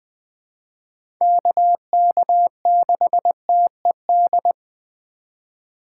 Morse code, heard as a keyed tone, about 20 wpm, 700 Hz. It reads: KK6TED
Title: USA Call Signs - 6 Characters - 8x Character Spacing - 20wpm